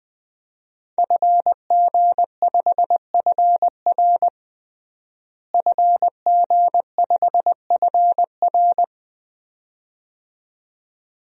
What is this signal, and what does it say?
Morse code, heard as a keyed tone, about 20 wpm, 700 Hz. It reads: FG5FR FG5FR
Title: DXCC-100 Call Signs - 5 Characters - Repeated 2x - 20wpm